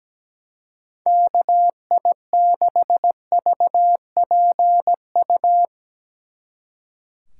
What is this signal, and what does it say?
Morse code, heard as a keyed tone, about 18 wpm, 700 Hz. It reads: KI6VPU
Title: Speed Racing - USA 2x3 Call Signs 17wpm